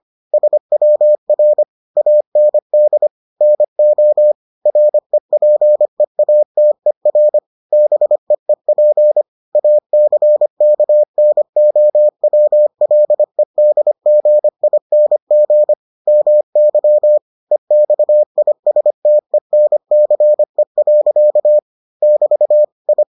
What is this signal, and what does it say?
Morse code, heard as a keyed tone, about 25 wpm, 600 Hz. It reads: SWR AND NO REPEATER BEEP ACKNOWLEDGING MY EXISTENCE. <BT> I